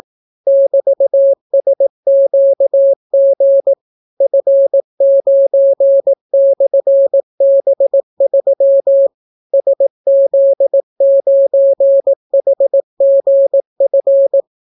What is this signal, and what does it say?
Morse code, heard as a keyed tone, about 18 wpm, 550 Hz. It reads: <BT>SQG F9/B3 SZ9HGF